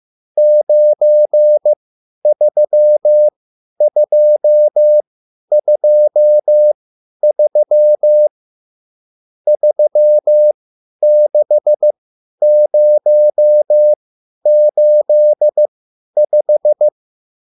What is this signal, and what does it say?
Morse code, heard as a keyed tone, about 15 wpm, spaced as for 10 wpm, 600 Hz. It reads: 93223 36085
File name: QSL / CW lesson